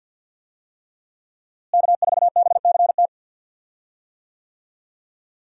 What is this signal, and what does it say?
Morse code, heard as a keyed tone, about 52 wpm, 700 Hz. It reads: K4BCT
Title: USA Call Signs - 5 Characters - Repeated 3x - 50wpm